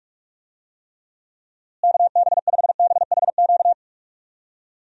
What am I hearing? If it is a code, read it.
Morse code, heard as a keyed tone, about 46 wpm, 700 Hz. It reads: KB5BHQ
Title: USA Call Signs - 6 Characters - Repeated 3x - 45wpm